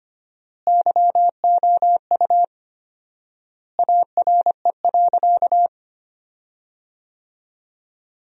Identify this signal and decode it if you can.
Morse code, heard as a keyed tone, about 25 wpm, 700 Hz. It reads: YOU ARE.
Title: 4x Word-Spacing - Sentences from Top 500 Words - 25wpm